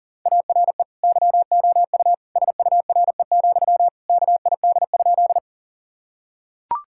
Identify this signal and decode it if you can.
Morse code, heard as a keyed tone, about 40 wpm, 700 Hz. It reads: ARE YOU SURE, KID?
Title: Sentences from Top 1600 Words 40wpm